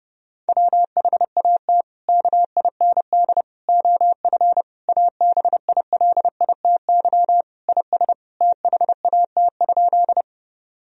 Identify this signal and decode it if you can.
Morse code, heard as a keyed tone, about 32 wpm, 700 Hz. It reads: WHAT KIND OF ABILITY IS THAT?
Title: Sentences from Top 1000 Words - Review Part 3 - 30wpm